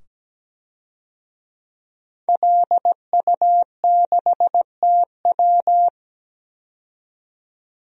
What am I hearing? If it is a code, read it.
Morse code, heard as a keyed tone, about 17 wpm, 700 Hz. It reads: LU6TW
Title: DXCC-100 Call Signs - 5 Characters - 17wpm